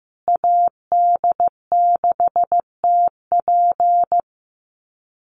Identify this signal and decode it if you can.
Morse code, heard as a keyed tone, about 15 wpm, 700 Hz. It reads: AD6TP
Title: USA Call Signs Repeated 3x 15wpm